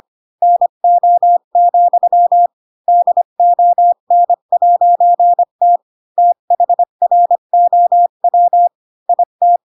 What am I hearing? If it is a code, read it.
Morse code, heard as a keyed tone, about 25 wpm, 700 Hz. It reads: NO, DON'T THROW IT
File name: Pinocchio / Pinocchio ch7 20 25 ik1hge